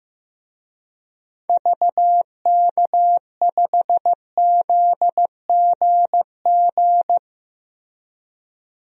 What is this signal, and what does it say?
Morse code, heard as a keyed tone, about 15 wpm, 700 Hz. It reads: VK5ZGG
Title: DXCC-100 Call Signs - 6 Characters - 15wpm